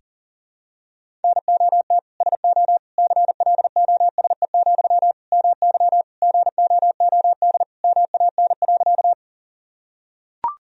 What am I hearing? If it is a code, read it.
Morse code, heard as a keyed tone, about 37 wpm, 700 Hz. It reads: NOT SO CLOSE, MY GOOD MAN.
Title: Sentences from Top 1000 Words - Review Part 1 - 40wpm